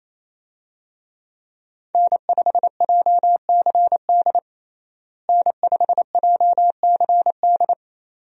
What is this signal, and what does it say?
Morse code, heard as a keyed tone, about 28 wpm, 700 Hz. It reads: N5JCD N5JCD
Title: USA Call Signs - 5 Characters - Repeated 2x - 28wpm